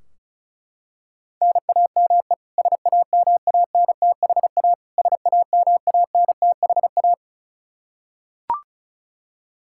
Morse code, heard as a keyed tone, about 36 wpm, 700 Hz. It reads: NAME SAMANTHA SAMANTHA
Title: Practice QSO - Exchange 1 - Beginning - 35wpm